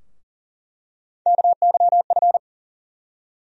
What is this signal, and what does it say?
Morse code, heard as a keyed tone, about 40 wpm, 700 Hz. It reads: KYF
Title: USA Call Sign Postfixes - 3 Letters 40wpm